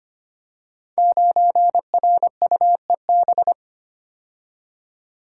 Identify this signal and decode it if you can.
Morse code, heard as a keyed tone, about 25 wpm, 700 Hz. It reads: 9RUEB
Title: Code Groups - 5 Letter:Numbers - 25wpm